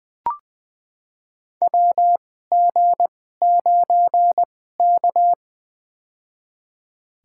Morse code, heard as a keyed tone, about 20 wpm, spaced as for 14 wpm, 700 Hz. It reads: WG9K
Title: USA Call Signs - 4 Characters - 2x Character Spacing - 20wpm